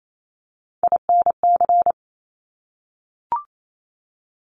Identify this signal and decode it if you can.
Morse code, heard as a keyed tone, about 28 wpm, 700 Hz. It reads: INC
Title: Top Three Letter Combinations in Words 28wpm